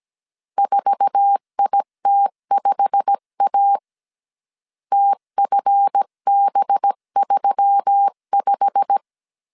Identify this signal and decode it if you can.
Morse code, heard as a keyed tone, about 17 wpm, 800 Hz. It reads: 4IT5A TFB35
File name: 1500C S B 16wpm